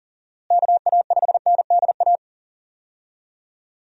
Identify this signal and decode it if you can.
Morse code, heard as a keyed tone, about 40 wpm, 700 Hz. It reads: KA5NDA
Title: USA Call Signs - 6 Characters - Repeated 3x - 40wpm